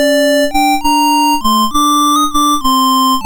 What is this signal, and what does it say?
There is no code.